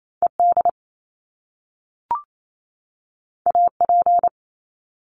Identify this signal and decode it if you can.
Morse code, heard as a keyed tone, about 28 wpm, 700 Hz. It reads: ED AP
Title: Top Two Letter Combinations in Words 28wpm